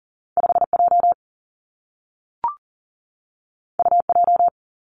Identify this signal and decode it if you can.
Morse code, heard as a keyed tone, about 41 wpm, 700 Hz. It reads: 5J UJ
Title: Two Challenging Letter-Number Combinations 40wpm